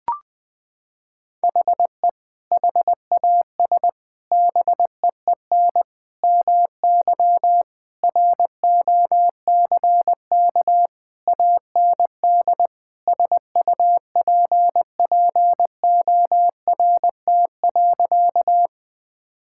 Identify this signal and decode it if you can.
Morse code, heard as a keyed tone, about 21 wpm, 700 Hz. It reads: HE HAS BEEN MY ROCK AND SUPPORT.
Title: Sentences from Top 1000 Words - Review Part 5 - 20wpm